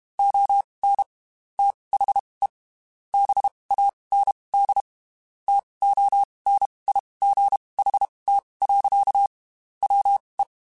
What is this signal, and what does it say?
Morse code, heard as a keyed tone, about 32 wpm, spaced as for 23 wpm, 800 Hz. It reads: ON THE BAND TONIGHT. WE
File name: QSOs / QSOs-mp3s / WD0FBT 1@30